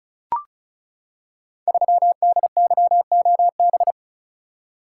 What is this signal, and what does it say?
Morse code, heard as a keyed tone, about 36 wpm, 700 Hz. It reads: 3DYOB
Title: Code Groups - 5 Letter:Numbers - 35wpm